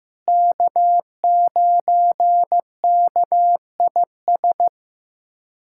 Morse code, heard as a keyed tone, about 15 wpm, 700 Hz. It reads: K9KIS
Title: USA Callsign DE Callsign - 5 Characters - 15wpm